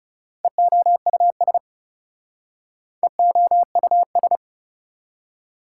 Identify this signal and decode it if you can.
Morse code, heard as a keyed tone, about 39 wpm, 690 Hz. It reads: EOUS EOUS